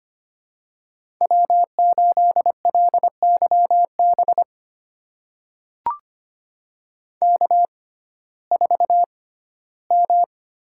Morse code, heard as a keyed tone, about 25 wpm, spaced as for 15 wpm, 700 Hz. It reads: W8LYB K4M
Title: USA Call Signs - 5 Characters - 6x Character Spacing - 25wpm